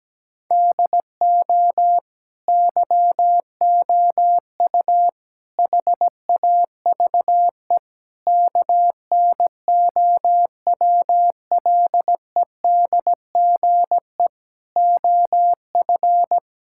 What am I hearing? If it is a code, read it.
Morse code, heard as a keyed tone, about 17 wpm, 700 Hz. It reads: DO YOU HAVE KNOWLEDGE OF
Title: Sentences from Top 1000 Words - Review Part 6 - 17wpm